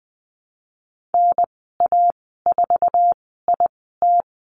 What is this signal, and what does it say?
Morse code, heard as a keyed tone, about 20 wpm, 700 Hz. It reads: N A 4 I T
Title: 048 Lesson 9g - Number 4 - Call Signs 20wpm